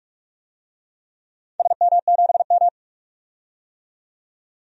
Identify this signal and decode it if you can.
Morse code, heard as a keyed tone, about 46 wpm, 700 Hz. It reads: SM7M
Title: DXCC-100 Call Signs - 4 Characters - 45wpm